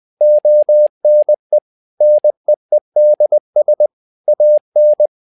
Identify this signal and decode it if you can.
Morse code, heard as a keyed tone, about 20 wpm, 600 Hz. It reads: ONE NEEDS AN
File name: learning center / tower trouble / tower trouble 20wpm